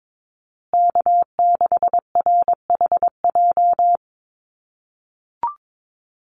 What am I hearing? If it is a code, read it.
Morse code, heard as a keyed tone, about 22 wpm, 700 Hz. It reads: K6RHJ
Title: USA Call Signs Repeated 3x 22wpm